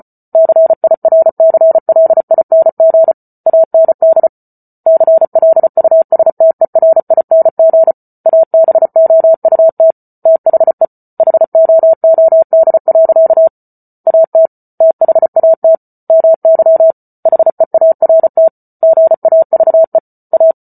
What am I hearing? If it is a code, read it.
Morse code, heard as a keyed tone, about 34 wpm, 650 Hz. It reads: CIRCLING AND CLUSTERING ABOUT THE HOOD. AT THAT MY HEART GAVE A